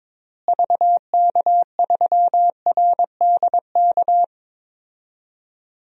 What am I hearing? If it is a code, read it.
Morse code, heard as a keyed tone, about 22 wpm, 700 Hz. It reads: VK3RDK
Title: DXCC-100 Call Signs - 6 Characters - Repeated 2x - 22wpm